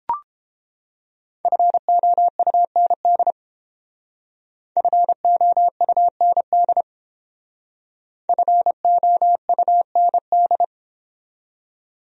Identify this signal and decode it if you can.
Morse code, heard as a keyed tone, about 31 wpm, 700 Hz. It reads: FOUND FOUND FOUND